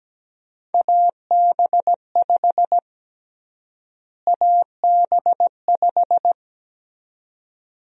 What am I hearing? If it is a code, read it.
Morse code, heard as a keyed tone, about 17 wpm, 700 Hz. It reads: AB5 AB5
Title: USA Call Sign Prefixes - Repeated 2x - 17wpm